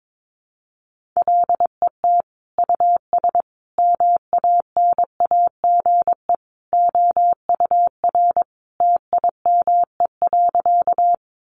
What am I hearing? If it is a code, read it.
Morse code, heard as a keyed tone, about 22 wpm, 700 Hz. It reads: LET US MANAGE OUR TIME.
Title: Sentences from Top 1400 Words 22wpm